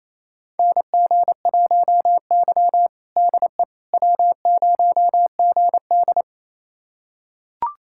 Morse code, heard as a keyed tone, about 28 wpm, 700 Hz. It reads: NG1Y DE W0GD
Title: USA Callsign DE Callsign - 4 Characters - 28wpm